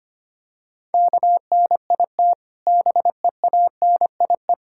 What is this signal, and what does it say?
Morse code, heard as a keyed tone, about 30 wpm, 700 Hz. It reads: KNIT BEANIE
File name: practice / 25wpm / Difficult Words - Speed-Racing with Hints - 25wpm